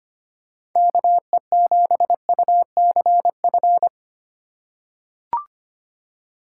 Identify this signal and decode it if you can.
Morse code, heard as a keyed tone, about 25 wpm, 700 Hz. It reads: KE7UCF
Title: USA Call Signs - 6 Characters - 3x Character Spacing - 25wpm